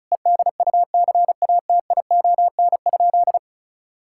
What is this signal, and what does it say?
Morse code, heard as a keyed tone, about 36 wpm, 700 Hz. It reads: EDUCATION?
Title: Sentences from Top 1000 Words - Review Part 6 - 35wpm